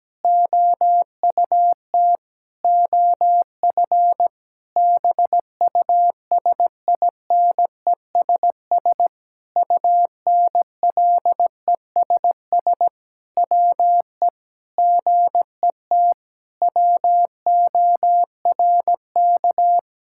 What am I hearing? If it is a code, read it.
Morse code, heard as a keyed tone, about 17 wpm, 700 Hz. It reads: OUT OF BUSINESS UNLESS WE GET WORK
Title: Sentences from Top 1000 Words - Review Part 6 - 17wpm